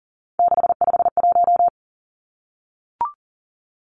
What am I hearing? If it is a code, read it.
Morse code, heard as a keyed tone, about 40 wpm, 700 Hz. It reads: 651
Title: Numbers - 3-Digit 40wpm